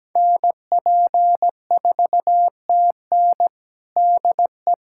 Morse code, heard as a keyed tone, about 17 wpm, 700 Hz. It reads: NP4TN DE
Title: USA Callsign DE Callsign 17wpm